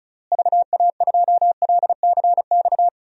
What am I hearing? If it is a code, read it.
Morse code, heard as a keyed tone, about 34 wpm, 700 Hz. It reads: VA2LCX
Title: DXCC-100 Call Signs - 6 Characters - Repeated 2x - 35wpm